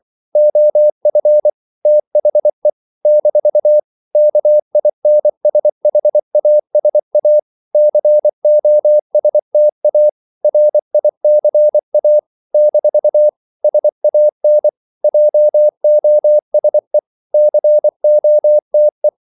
Morse code, heard as a keyed tone, about 24 wpm, 600 Hz. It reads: OF THE - KINSHASA COSTA RICA - SAN JOSE COTE
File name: CW / Practice / NationalCapitals24